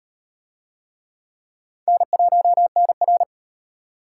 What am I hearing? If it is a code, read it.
Morse code, heard as a keyed tone, about 38 wpm, 700 Hz. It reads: N1NR